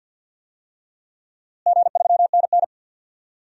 Morse code, heard as a keyed tone, about 52 wpm, 700 Hz. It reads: G3NN
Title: DXCC-100 Call Signs - 4 Characters - Repeated 2x - 50wpm